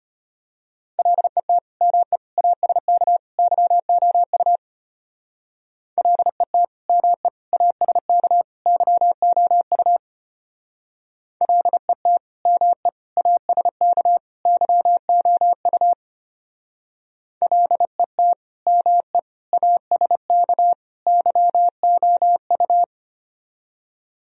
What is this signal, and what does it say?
Morse code, heard as a keyed tone, about 30 wpm, 700 Hz. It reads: LET ME ASK YOU LET ME ASK YOU LET ME ASK YOU LET ME ASK YOU
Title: Speed-Racing - Sets of 4 Words 25wpm